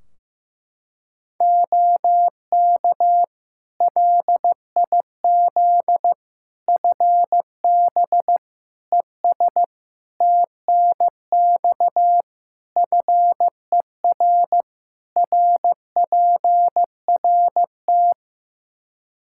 Morse code, heard as a keyed tone, about 15 wpm, 700 Hz. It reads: OK LIZ FB ES TNX FER RPRT
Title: K1USN SST QSO Practice 15wpm